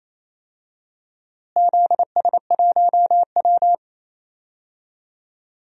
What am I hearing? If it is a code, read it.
Morse code, heard as a keyed tone, about 28 wpm, 700 Hz. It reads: ZS1W
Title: DXCC-100 Call Signs - 4 Characters - Repeated 2x - 28wpm